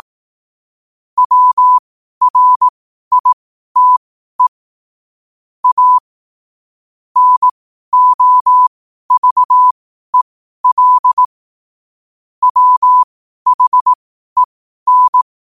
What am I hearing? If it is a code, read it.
Morse code, heard as a keyed tone, about 18 wpm, spaced as for 12 wpm, 1000 Hz. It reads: WRITE A NOVEL WHEN
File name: qotdfiles / 12wpm